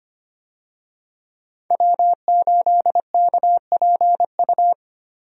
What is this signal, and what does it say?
Morse code, heard as a keyed tone, about 25 wpm, 700 Hz. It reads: W8KPU
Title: USA Call Signs - 5 Characters - Repeated 2x - 25wpm